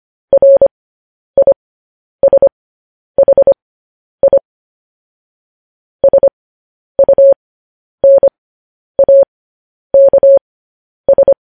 Morse code, heard as a keyed tone, about 25 wpm, spaced as for 10 wpm, 550 Hz. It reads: RISHI SUNAKS
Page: news headlines in morse code at 10 wpm